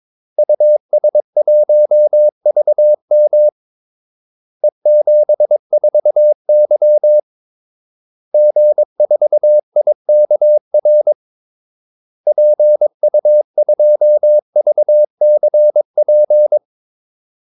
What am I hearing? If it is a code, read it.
Morse code, heard as a keyed tone, about 22 wpm, 600 Hz. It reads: US1VM E74Y G4IKR PU2VCP
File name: Callsigns-22wpm